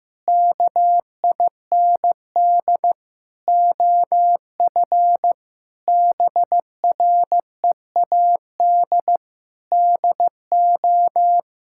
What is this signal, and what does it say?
Morse code, heard as a keyed tone, about 15 wpm, 700 Hz. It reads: KIND OF BREAD DO
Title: Sentences from Top 1600 Words 15wpm